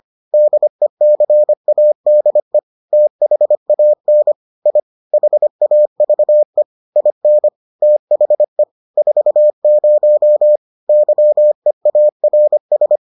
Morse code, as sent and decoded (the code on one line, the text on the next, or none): -.. -.-. .- -.. / - .... .- -. / .. / .... .- ...- . / .. -. / - .... / ....- ----- / -.-- . .- .-. ...
DECADE THAN I HAVE IN THE 40 YEARS